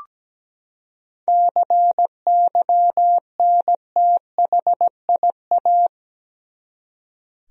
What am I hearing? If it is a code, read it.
Morse code, heard as a keyed tone, about 17 wpm, 700 Hz. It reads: CYNTHIA
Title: USA Common Names - 17wpm